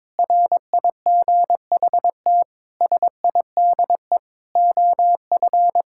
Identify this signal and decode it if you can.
Morse code, heard as a keyed tone, about 22 wpm, 700 Hz. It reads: RIGHT SIDE OF
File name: practice / 22wpm / Sets of 8 Words - 22wpm